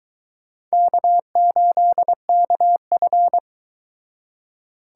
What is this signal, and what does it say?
Morse code, heard as a keyed tone, about 24 wpm, 700 Hz. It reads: K8KF